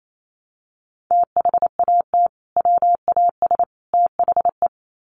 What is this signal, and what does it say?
Morse code, heard as a keyed tone, about 28 wpm, 700 Hz. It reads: THAT WAS THE
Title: Sets of 3 Words 28wpm